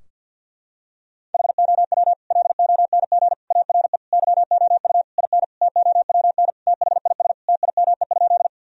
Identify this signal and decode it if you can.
Morse code, heard as a keyed tone, about 52 wpm, 700 Hz. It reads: HOW LONG ARE YOU IN TOWN THIS TIME?
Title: Sentences from Top 1000 Words - Review Part 2 - 50wpm